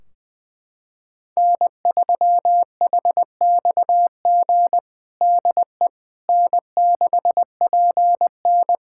Morse code, heard as a keyed tone, about 21 wpm, 700 Hz. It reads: N3HXG DE N6PN
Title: K1USN SST QSO Practice 20wpm